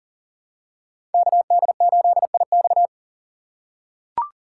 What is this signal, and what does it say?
Morse code, heard as a keyed tone, about 40 wpm, 700 Hz. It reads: KD8IX
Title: USA Call Signs - 5 Characters - 2x Character Spacing - 40wpm